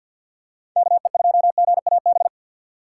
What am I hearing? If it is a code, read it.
Morse code, heard as a keyed tone, about 53 wpm, 700 Hz. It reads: KE2GAB
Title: USA Call Signs - 6 Characters - 8x Character Spacing - 50wpm